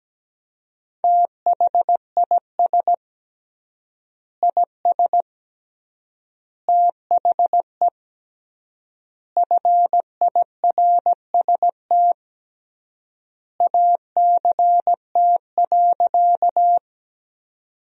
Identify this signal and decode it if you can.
Morse code, heard as a keyed tone, about 17 wpm, 700 Hz. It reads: THIS IS THE FIRST ACT.
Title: 3x Word-Spacing - Sentences from Top 200 Words - 17wpm